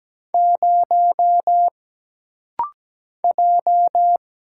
Morse code, heard as a keyed tone, about 17 wpm, 700 Hz. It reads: 0 J
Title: Single Letter-Number - Rapid-Fire 17wpm